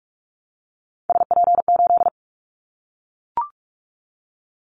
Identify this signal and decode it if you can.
Morse code, heard as a keyed tone, about 44 wpm, 700 Hz. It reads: SP8
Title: Callsign Prefixes for Top 25 DXCC - Repeated 3x - 45wpm